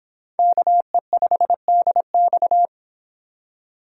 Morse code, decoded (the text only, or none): KE5DX